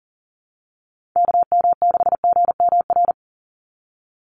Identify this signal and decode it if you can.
Morse code, heard as a keyed tone, about 40 wpm, 700 Hz. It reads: KM6GMR
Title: USA Call Signs 40wpm